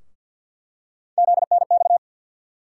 Morse code, heard as a keyed tone, about 48 wpm, 700 Hz. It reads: ZNX
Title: USA Call Sign Postfixes - 3 Letters 50wpm